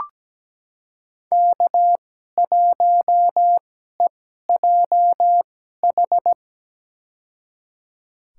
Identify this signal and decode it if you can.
Morse code, heard as a keyed tone, about 17 wpm, spaced as for 12 wpm, 700 Hz. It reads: K1EJH
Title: USA Call Signs - 2x Character Spacing - 17wpm